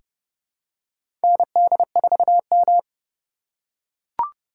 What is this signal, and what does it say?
Morse code, heard as a keyed tone, about 30 wpm, 700 Hz. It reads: ND4M
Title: USA Call Signs - 6x Character Spacing - 30wpm